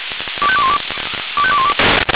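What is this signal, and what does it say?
Five-tone selective call (ZVEI), heard as quick runs of five tones, tones of about 70 ms. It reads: 25212 25212
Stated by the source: ZVEI five-tone selective call received by SDR.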